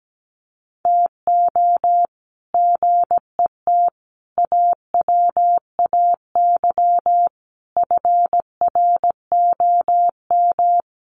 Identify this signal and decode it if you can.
Morse code, heard as a keyed tone, about 17 wpm, 700 Hz. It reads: TO GET AWAY FROM
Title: Sets of 5 Words Encore2 17wpm